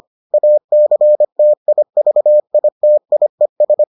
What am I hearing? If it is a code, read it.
Morse code, heard as a keyed tone, about 25 wpm, 600 Hz. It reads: ACTIVITIES